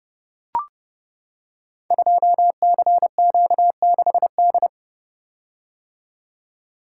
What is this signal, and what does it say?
Morse code, heard as a keyed tone, about 30 wpm, 700 Hz. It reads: 2CQ6D
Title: Code Groups - 5 Letter:Numbers - 30wpm